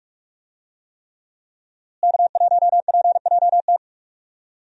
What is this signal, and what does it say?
Morse code, heard as a keyed tone, about 44 wpm, 700 Hz. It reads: K1PJT
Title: USA Call Signs - 5 Characters - Repeated 3x - 45wpm